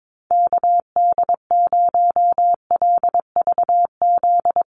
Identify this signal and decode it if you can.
Morse code, heard as a keyed tone, about 22 wpm, 700 Hz. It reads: KD0LVZ
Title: USA Call Signs Repeated 3x 22wpm